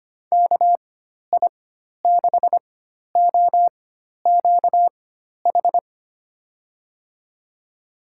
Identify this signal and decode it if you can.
Morse code, heard as a keyed tone, about 25 wpm, spaced as for 12 wpm, 700 Hz. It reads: KI6OQH
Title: USA Call Signs - 6 Characters - 4x Character Spacing - 25wpm